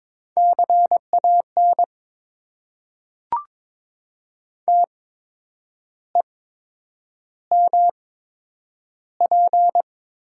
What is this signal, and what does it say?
Morse code, heard as a keyed tone, about 21 wpm, spaced as for 8 wpm, 700 Hz. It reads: CAN TEMP
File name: Top 90 QSO Elements - 8x Character Spacing - 22wpm